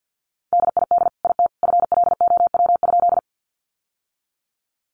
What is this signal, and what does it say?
Morse code, heard as a keyed tone, about 50 wpm, 700 Hz. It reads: DID IT FLOW?
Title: Sentences from Top 1200 Words 50wpm